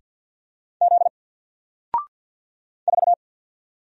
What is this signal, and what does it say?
Morse code, heard as a keyed tone, about 48 wpm, 700 Hz. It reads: Z 4
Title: Single Letter-Number Alt 50wpm